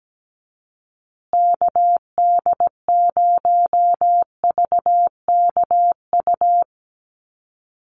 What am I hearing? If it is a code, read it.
Morse code, heard as a keyed tone, about 17 wpm, 700 Hz. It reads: KD0VKU
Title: USA Call Signs Repeated 2x 17wpm